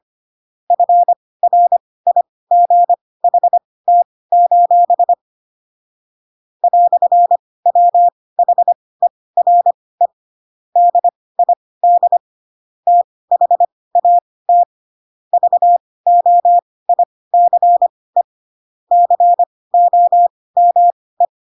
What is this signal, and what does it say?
Morse code, heard as a keyed tone, about 25 wpm, spaced as for 18 wpm, 700 Hz. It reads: FRIGHT: "WHERE DID THAT VOICE COME
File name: Pinocchio / Pinocchio ch1 12 25 ik1hge